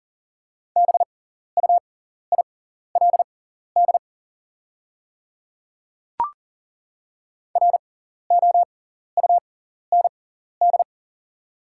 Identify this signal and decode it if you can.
Morse code, heard as a keyed tone, about 37 wpm, spaced as for 14 wpm, 700 Hz. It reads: BUILD ROUND
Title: 6x Character Spacing - Top 300 Words - 40wpm